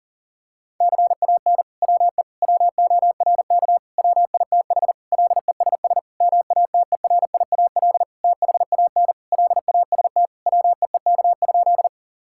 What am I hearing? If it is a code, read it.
Morse code, heard as a keyed tone, about 40 wpm, 700 Hz. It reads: CAN WE WORK WITH LESS MATERIAL THAN LAST WEEK?
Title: Sentences from Top 1000 Words - Review Part 5 - 40wpm